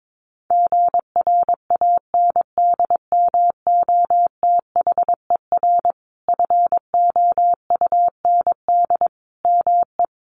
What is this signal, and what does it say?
Morse code, heard as a keyed tone, about 22 wpm, 700 Hz. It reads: GRANDMOTHER FOUND ME
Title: Sentences from Top 1200 Words 22wpm